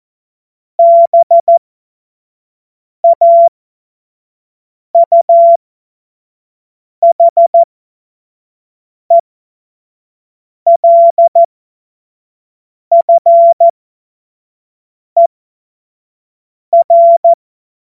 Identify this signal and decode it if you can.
Morse code, heard as a keyed tone, about 14 wpm, spaced as for 5 wpm, 680 Hz. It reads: BAUHELFER
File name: Morsekurs / Witze / Witz 31.Oktober 2023 25 F70